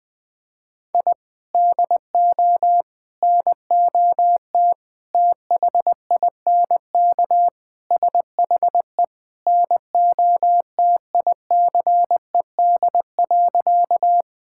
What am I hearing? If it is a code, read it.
Morse code, heard as a keyed tone, about 20 wpm, 700 Hz. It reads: I DO NOT THINK SHE NOTICED.
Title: Sentences with 5-Word Phrases - 20wpm